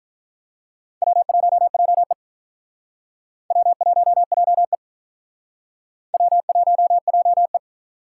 Morse code, heard as a keyed tone, about 48 wpm, 700 Hz. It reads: W1JE W1JE W1JE